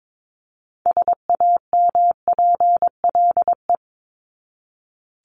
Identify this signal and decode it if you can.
Morse code, heard as a keyed tone, about 22 wpm, 700 Hz. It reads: SAMPLE
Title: Top 1501-1750 Words 22wpm